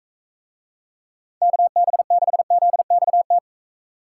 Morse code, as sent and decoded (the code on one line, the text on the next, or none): -.- -... -.... --.. -..- -
KB6ZXT